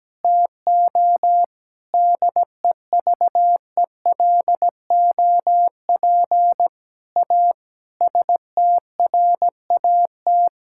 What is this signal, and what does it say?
Morse code, heard as a keyed tone, about 17 wpm, 700 Hz. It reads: TO DEVELOP A STRAT
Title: Sentences from Top 1000 Words - Review Part 5 - 17wpm